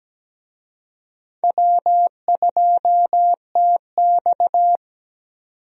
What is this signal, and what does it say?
Morse code, heard as a keyed tone, about 17 wpm, 700 Hz. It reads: W2TX
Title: USA Call Signs - 4 Characters - 17wpm